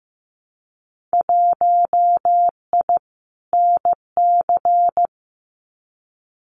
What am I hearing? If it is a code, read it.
Morse code, heard as a keyed tone, about 15 wpm, 700 Hz. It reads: 1I NC
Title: Winter Field Day Exchanges 15wpm